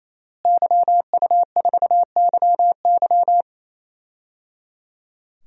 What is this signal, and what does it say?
Morse code, heard as a keyed tone, about 28 wpm, 700 Hz. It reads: YU4YY
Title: DXCC-100 Call Signs - 5 Characters - Repeated 3x - 28wpm